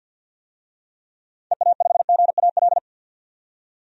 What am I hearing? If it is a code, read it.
Morse code, heard as a keyed tone, about 52 wpm, 700 Hz. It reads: EA5GAL
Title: DXCC-100 Call Signs - 6 Characters - 50wpm